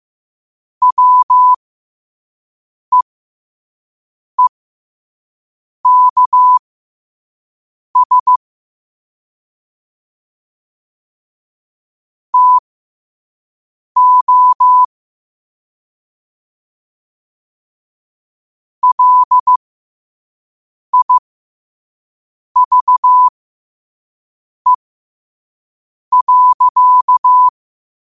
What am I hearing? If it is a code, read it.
Morse code, heard as a keyed tone, about 15 wpm, spaced as for 5 wpm, 1000 Hz. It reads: WEEKS TO LIVE.
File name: qotdfiles / 5wpm